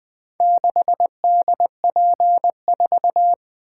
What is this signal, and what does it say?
Morse code, heard as a keyed tone, about 20 wpm, 700 Hz. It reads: 6DP4
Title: Four Letter-Number Combinations - 20wpm